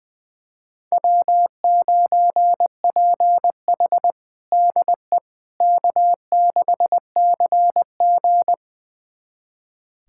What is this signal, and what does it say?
Morse code, heard as a keyed tone, about 20 wpm, 700 Hz. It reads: W9PH DE K6CG
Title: USA Callsign DE Callsign - 4 Characters - 20wpm